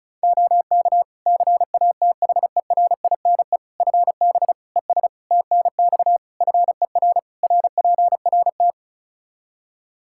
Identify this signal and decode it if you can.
Morse code, heard as a keyed tone, about 34 wpm, 700 Hz. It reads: OK CATHERINE FB ES TNX FER RPRT
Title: Practice QSO - Exchange 2 - Info - 35wpm